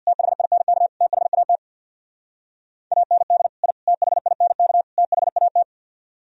Morse code, heard as a keyed tone, about 62 wpm, 700 Hz. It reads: THINK THAT AND I THINK THAT
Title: Speed-Racing - Sets of 4 Words 50wpm